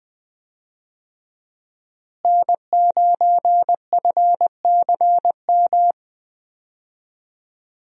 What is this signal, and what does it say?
Morse code, heard as a keyed tone, about 20 wpm, 700 Hz. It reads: N9FCM
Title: USA Call Signs - 5 Characters - Repeated 2x - 20wpm